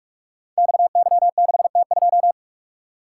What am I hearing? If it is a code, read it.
Morse code, heard as a keyed tone, about 44 wpm, 700 Hz. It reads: XY6T2
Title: Code Groups - 5 Letter:Numbers - 45wpm